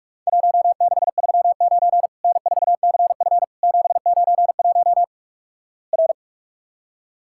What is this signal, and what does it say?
Morse code, heard as a keyed tone, about 46 wpm, 700 Hz. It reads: 1639 N4CF 791 R
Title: Worked All Europe DX Contest - QTC Exchange - 45wpm